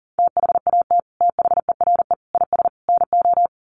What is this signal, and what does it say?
Morse code, heard as a keyed tone, about 40 wpm, 700 Hz. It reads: THAT THERE IS NO
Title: Sets of 4 Words 40wpm